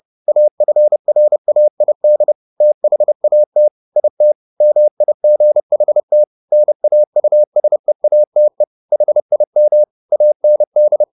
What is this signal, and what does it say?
Morse code, heard as a keyed tone, about 30 wpm, 600 Hz. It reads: AFRAID THAT IT MIGHT NAUSEATE HIM AND